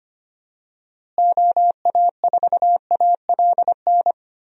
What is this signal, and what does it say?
Morse code, heard as a keyed tone, about 25 wpm, 700 Hz. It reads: OA4ALN
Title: USA Call Signs - 6 Characters - Repeated 2x - 25wpm